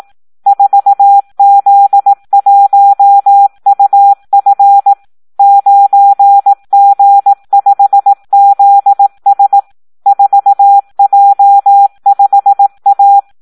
Morse code, heard as a keyed tone, about 18 wpm, 800 Hz. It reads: 4Z1UF 9G5ZS 4J5A
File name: Audio practica CW / SD 1 18